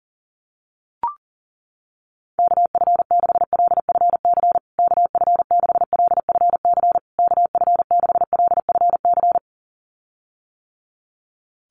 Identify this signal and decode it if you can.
Morse code, heard as a keyed tone, about 40 wpm, 700 Hz. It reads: KF6LFC KF6LFC KF6LFC
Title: USA Call Signs Repeated 3x 40wpm